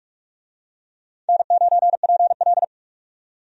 Morse code, heard as a keyed tone, about 42 wpm, 700 Hz. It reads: N9PL
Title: USA Call Signs - 4 Characters - 45wpm